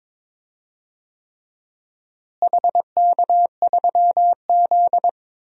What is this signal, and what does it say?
Morse code, heard as a keyed tone, about 22 wpm, 700 Hz. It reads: HK3Z
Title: DXCC-100 Call Signs - 4 Characters - Repeated 3x - 22wpm